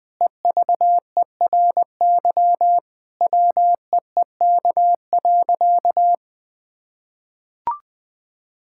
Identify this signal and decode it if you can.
Morse code, heard as a keyed tone, about 20 wpm, 700 Hz. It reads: EVERY WEEK.
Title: Sentences from Top 1000 Words - Review Part 5 - 20wpm